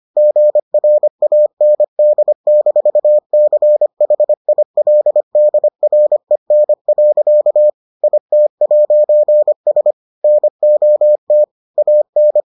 Morse code, heard as a keyed tone, about 25 wpm, 600 Hz. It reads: GRAND-CHILDREN. IT'S NOT AN